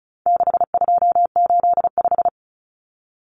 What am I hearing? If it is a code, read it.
Morse code, heard as a keyed tone, about 34 wpm, 700 Hz. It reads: 6285
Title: Common Cut Numbers - 4-Digit 35wpm